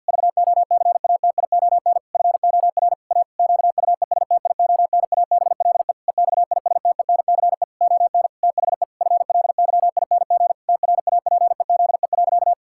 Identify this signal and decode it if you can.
Morse code, heard as a keyed tone, about 50 wpm, 700 Hz. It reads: VOCATION FOR A QUESTIONABLE EXISTENCE ON THE FLYING TRAPEZE.
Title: Pangram Challenge v2 - 50wpm